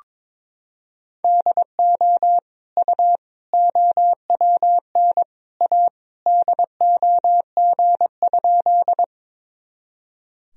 Morse code, heard as a keyed tone, about 22 wpm, 700 Hz. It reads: DO U OWN A DOG?
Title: Sentences from Top 400 Words and Abbreviations 22wpm